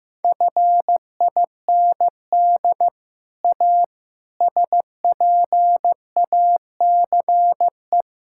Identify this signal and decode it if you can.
Morse code, heard as a keyed tone, about 15 wpm, 700 Hz. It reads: FIND A SPACE
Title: Sentences from Top 1000 Words - Review Part 5 - 15wpm